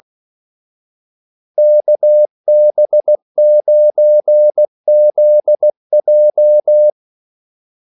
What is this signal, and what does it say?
Morse code, heard as a keyed tone, about 16 wpm, 600 Hz. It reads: KB9ZJ
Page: Kaldesignaler 16 wpm